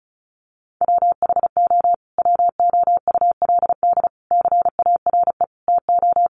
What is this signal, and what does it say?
Morse code, heard as a keyed tone, about 34 wpm, 700 Hz. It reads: WHO WOULD CARE TO
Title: Sentences from Top 1500 Words 35wpm